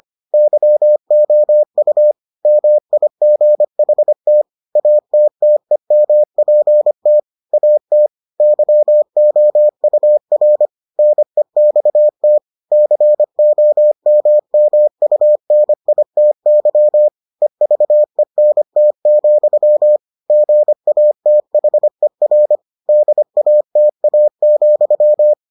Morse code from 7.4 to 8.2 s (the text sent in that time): AT